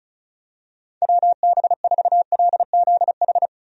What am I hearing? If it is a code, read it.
Morse code, heard as a keyed tone, about 34 wpm, 700 Hz. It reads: WB4LZH
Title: USA Call Signs - 6 Characters - Repeated 3x - 35wpm